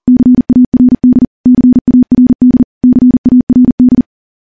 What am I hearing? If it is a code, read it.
Morse code, heard as a keyed tone, about 43 wpm, 250 Hz. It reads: CARD CARD CARD
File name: morse / morse-training-40WPM